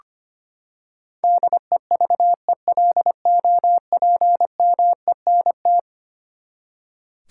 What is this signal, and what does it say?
Morse code, heard as a keyed tone, about 25 wpm, 700 Hz. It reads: DEVELOPMENT
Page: morse-code-ninja-practice-25wpm